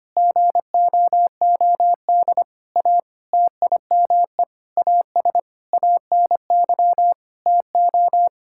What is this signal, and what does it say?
Morse code, heard as a keyed tone, about 25 wpm, 700 Hz. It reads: GOOD A TIME AS ANY TO
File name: Sets of 9 Words - 25wpm